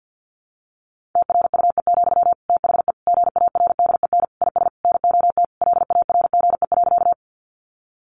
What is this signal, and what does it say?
Morse code, heard as a keyed tone, about 50 wpm, 700 Hz. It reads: TRUE, THE GARDEN IS NOT LARGE.
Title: Sentences from Top 700 Words 50wpm